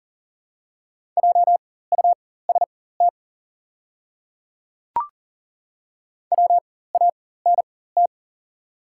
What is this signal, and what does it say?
Morse code, heard as a keyed tone, about 40 wpm, spaced as for 19 wpm, 700 Hz. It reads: JUST WANT
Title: 4x Character Spacing - Top 200 Words - 40wpm